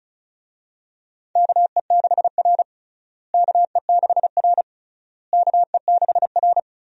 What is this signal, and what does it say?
Morse code, heard as a keyed tone, about 36 wpm, 700 Hz. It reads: KE6R KE6R KE6R
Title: USA Call Signs - 4 Characters - Repeated 3x - 35wpm